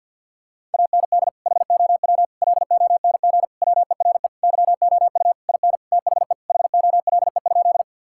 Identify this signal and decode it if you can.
Morse code, heard as a keyed tone, about 52 wpm, 700 Hz. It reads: AND HOW LONG WERE YOU IN THE HOLE?
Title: Sentences from Top 1600 Words 50wpm